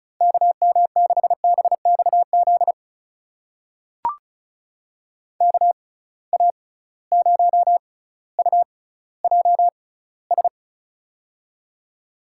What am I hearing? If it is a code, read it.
Morse code, heard as a keyed tone, about 34 wpm, spaced as for 18 wpm, 700 Hz. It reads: KM6BXZ KA0UJS
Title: USA Call Signs - 6 Characters - 6x Character Spacing - 35wpm